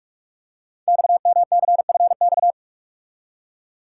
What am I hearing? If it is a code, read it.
Morse code, heard as a keyed tone, about 44 wpm, 700 Hz. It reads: XMCFX
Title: Code Groups - 5 Letters - 45wpm